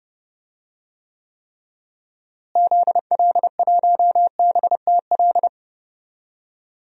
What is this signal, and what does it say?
Morse code, heard as a keyed tone, about 31 wpm, 700 Hz. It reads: ZL1BTL
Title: DXCC-100 Call Signs - 6 Characters - Repeated 3x - 30wpm